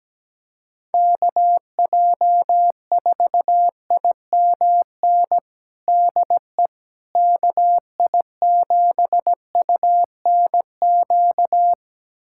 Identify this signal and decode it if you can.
Morse code, heard as a keyed tone, about 17 wpm, 700 Hz. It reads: KJ4IMN DE KI7UNQ
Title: USA Callsign DE Callsign - 6 Characters - 17wpm